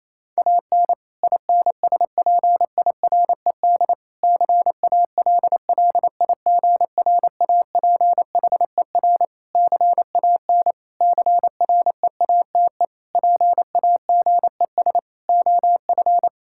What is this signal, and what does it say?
Morse code, heard as a keyed tone, about 28 wpm, 700 Hz. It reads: AN INSPIRED CALLIGRAPHER CAN CREATE PAGES OF
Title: Pangram Challenge v2 - 28wpm